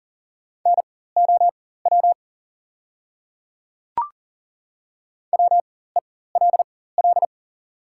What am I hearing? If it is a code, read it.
Morse code, heard as a keyed tone, about 42 wpm, spaced as for 19 wpm, 690 Hz. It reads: NOW WELL
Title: 4x Character Spacing - Top 300 Words - 40wpm